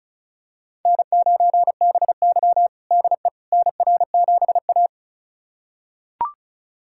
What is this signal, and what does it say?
Morse code, heard as a keyed tone, about 36 wpm, 690 Hz. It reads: N9BY DE NR7A
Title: USA Callsign DE Callsign - 4 Characters - 35wpm